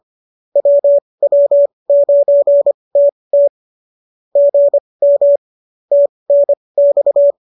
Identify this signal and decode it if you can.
Morse code, heard as a keyed tone, about 25 wpm, spaced as for 20 wpm, 570 Hz. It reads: WW9TT GM TNX
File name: qso223 20